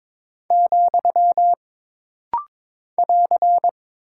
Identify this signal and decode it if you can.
Morse code, heard as a keyed tone, about 22 wpm, 700 Hz. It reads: , <AR>
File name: practice / 22wpm / Single Punctuation-Prosign - Rapid-Fire - 22wpm